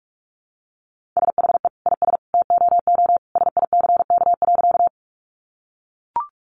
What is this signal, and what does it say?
Morse code, heard as a keyed tone, about 44 wpm, 700 Hz. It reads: SHE IS TOO SICK.
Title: Easy Phrases 45wpm